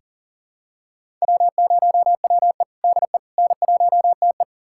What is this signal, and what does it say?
Morse code, heard as a keyed tone, about 40 wpm, 700 Hz. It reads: W0WE DE N1TE
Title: USA Callsign DE Callsign - 4 Characters - 40wpm